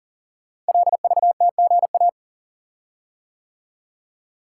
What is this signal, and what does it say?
Morse code, heard as a keyed tone, about 39 wpm, 690 Hz. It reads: LVTGA
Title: Code Groups - 5 Letters - 40wpm